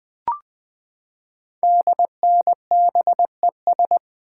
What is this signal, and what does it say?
Morse code, heard as a keyed tone, about 20 wpm, 700 Hz. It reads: DNBES
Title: Code Groups - 5 Letters - 20wpm